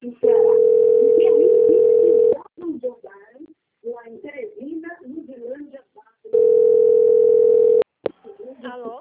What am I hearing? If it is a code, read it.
Ringback tone (North America), 440 + 480 Hz, with speech.
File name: out- - - - - .40654